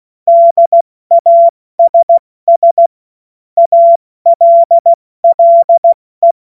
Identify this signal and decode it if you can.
Morse code, heard as a keyed tone, about 16 wpm, 680 Hz. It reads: DASS ALLE
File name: Morsekurs / Witze / Witz 12. April 2022 70 F80